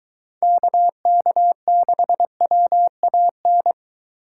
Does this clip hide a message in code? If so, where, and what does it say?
Yes: Morse code from 0.4 to 3.7 s, heard as a keyed tone, about 23 wpm, 700 Hz. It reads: KK6WAN